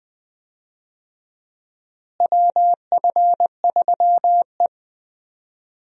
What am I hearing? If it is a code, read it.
Morse code, heard as a keyed tone, about 21 wpm, 700 Hz. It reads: WF3E
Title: USA Call Signs - 4 Characters - Repeated 2x - 20wpm